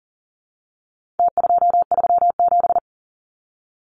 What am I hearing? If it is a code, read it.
Morse code, heard as a keyed tone, about 40 wpm, 700 Hz. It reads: T237
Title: Numbers - Contest Style - 4-Digit 40wpm